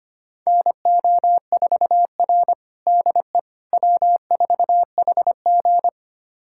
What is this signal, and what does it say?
Morse code, heard as a keyed tone, about 25 wpm, 700 Hz. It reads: NO4R DE W4HG
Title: USA Callsign DE Callsign - 4 Characters - 25wpm